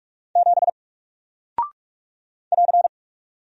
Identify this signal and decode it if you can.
Morse code, heard as a keyed tone, about 48 wpm, 700 Hz. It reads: 7 <AR>
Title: Single Character (Letter:Number:Punctuation:Prosign) Alt 45wpm